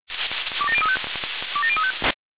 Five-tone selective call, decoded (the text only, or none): 27035 27035